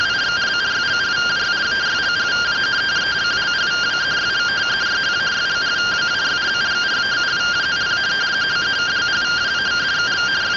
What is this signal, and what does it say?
Two-tone teleprinter signal, about 170 Hz shift.
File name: findings / 81 axial iodine 262.gen-25 rtty rx